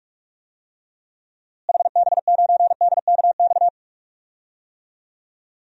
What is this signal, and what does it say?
Morse code, heard as a keyed tone, about 43 wpm, 700 Hz. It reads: HB9DKX
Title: DXCC-100 Call Signs - 6 Characters - Repeated 3x - 45wpm